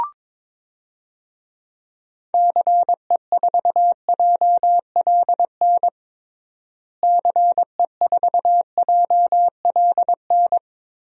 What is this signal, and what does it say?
Morse code, heard as a keyed tone, about 22 wpm, 700 Hz. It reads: CE4JLN CE4JLN
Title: DXCC-100 Call Signs - 6 Characters - Repeated 3x - 22wpm